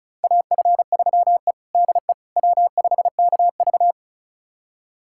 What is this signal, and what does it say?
Morse code, heard as a keyed tone, about 36 wpm, 700 Hz. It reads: AF3E DE W5KV
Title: USA Callsign DE Callsign - 4 Characters - 35wpm